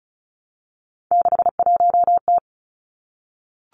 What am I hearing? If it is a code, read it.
Morse code, heard as a keyed tone, about 36 wpm, 700 Hz. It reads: 61T
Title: Three Challenging Letter-Number Combinations 35wpm